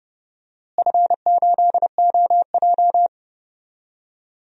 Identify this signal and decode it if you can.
Morse code, heard as a keyed tone, about 30 wpm, 700 Hz. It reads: F8OJ
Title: DXCC-100 Call Signs - 4 Characters - Repeated 3x - 30wpm